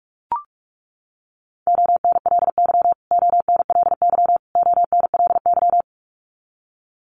Morse code, heard as a keyed tone, about 44 wpm, 700 Hz. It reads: ONLY ONLY ONLY
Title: Top 300 Words Repeated 45wpm